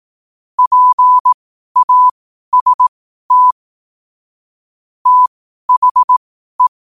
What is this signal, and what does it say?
Morse code, heard as a keyed tone, about 18 wpm, spaced as for 12 wpm, 1000 Hz. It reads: PAST THE
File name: qotdfiles / 12wpm